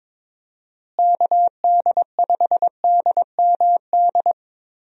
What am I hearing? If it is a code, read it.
Morse code, heard as a keyed tone, about 21 wpm, 700 Hz. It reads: KD5DMD
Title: USA Call Signs - 6 Characters - Repeated 2x - 22wpm